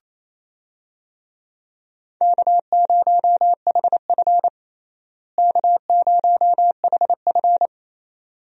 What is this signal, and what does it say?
Morse code, heard as a keyed tone, about 28 wpm, 700 Hz. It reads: K0HF K0HF
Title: USA Call Signs - 4 Characters - Repeated 2x - 28wpm